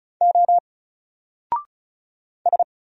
Morse code, heard as a keyed tone, about 33 wpm, 700 Hz. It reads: O S
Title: Single Letter-Number 35wpm